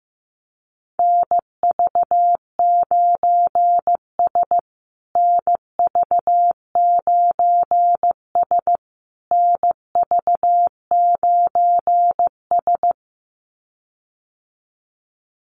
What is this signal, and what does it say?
Morse code, heard as a keyed tone, about 15 wpm, 700 Hz. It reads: NV9S NV9S NV9S
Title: USA Call Signs Repeated 3x 15wpm